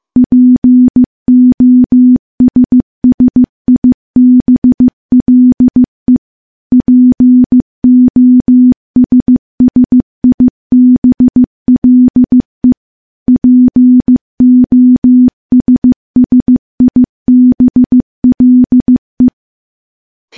Morse code, heard as a keyed tone, about 15 wpm, 250 Hz. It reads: POSSIBLE POSSIBLE POSSIBLE
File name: morse-training-15WPM